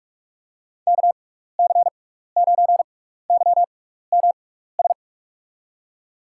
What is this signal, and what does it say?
Morse code, heard as a keyed tone, about 47 wpm, spaced as for 16 wpm, 700 Hz. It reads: KC9YMS
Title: USA Call Signs - 6x Character Spacing - 45wpm